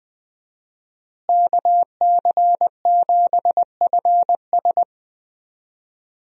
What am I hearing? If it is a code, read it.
Morse code, heard as a keyed tone, about 20 wpm, 700 Hz. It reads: KC7FS
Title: USA Call Signs - 5 Characters - Repeated 3x - 20wpm